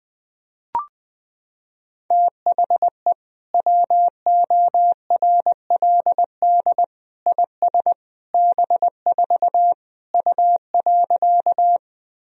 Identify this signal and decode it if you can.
Morse code, heard as a keyed tone, about 20 wpm, 700 Hz. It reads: THE WORLD IS B4 U.
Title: Sentences from Top 200 Words and Abbreviations 20wpm